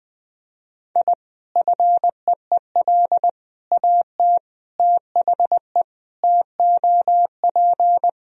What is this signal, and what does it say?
Morse code, heard as a keyed tone, about 21 wpm, 700 Hz. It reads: I FEEL AT THE TOP
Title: Sentences from Top 1000 Words - Review Part 2 - 20wpm